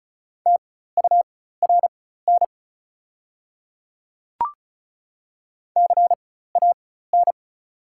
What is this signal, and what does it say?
Morse code, heard as a keyed tone, about 36 wpm, spaced as for 16 wpm, 700 Hz. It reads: TURN CAN
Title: 4x Character Spacing - Top 200 Words - 35wpm